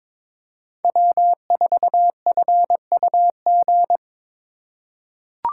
Morse code, heard as a keyed tone, about 22 wpm, 700 Hz. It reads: W4FUG
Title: USA Call Signs - 5 Characters - 3x Character Spacing - 22wpm